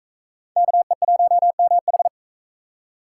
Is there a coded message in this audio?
Yes: Morse code from 0.6 to 2.1 s, heard as a keyed tone, about 40 wpm, 700 Hz. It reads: KE1MH